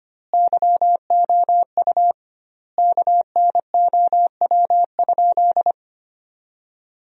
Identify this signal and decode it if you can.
Morse code, heard as a keyed tone, about 25 wpm, 700 Hz. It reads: YOU KNOW?
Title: 2x Word-Spacing - Sentences from Top 200 Words - 25wpm